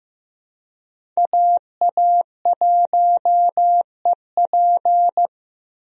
Morse code, heard as a keyed tone, about 15 wpm, 700 Hz. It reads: AA1EP
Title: USA Call Signs - 5 Characters - Repeated 2x - 15wpm